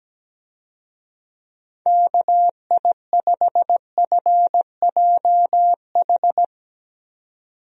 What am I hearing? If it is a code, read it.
Morse code, heard as a keyed tone, about 17 wpm, 700 Hz. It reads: KI5FJH
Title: USA Call Signs - 6 Characters - 17wpm